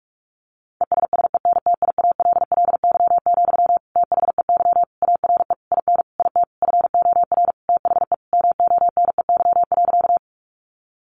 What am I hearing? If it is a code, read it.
Morse code, heard as a keyed tone, about 44 wpm, 700 Hz. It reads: ESSENTIALLY, THEY ARE IN IT FOR THE MONEY.
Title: Sentences from Top 1300 Words 45wpm